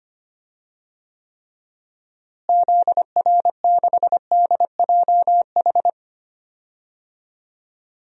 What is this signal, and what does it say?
Morse code, heard as a keyed tone, about 25 wpm, 700 Hz. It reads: ZR6DJH
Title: DXCC-100 Call Signs - 6 Characters - Repeated 2x - 25wpm